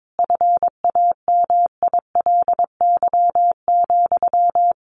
Morse code, heard as a keyed tone, about 22 wpm, 700 Hz. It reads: FAMILY,
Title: Sentences from Top 1100 Words 22wpm